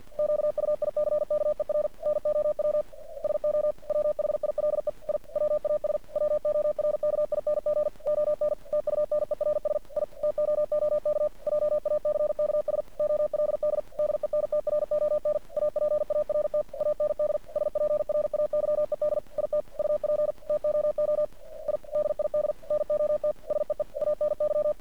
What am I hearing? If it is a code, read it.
Morse code, heard as a keyed tone, about 48 wpm, 600 Hz. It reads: QUICKER NOW SO WHILE I WAS WORKING ON TUNERS I TOOK JACKS OLD DENTRON APART AND SPRAYED IT UP TOO I DID NOT SEE ANY
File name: archived / CRSnet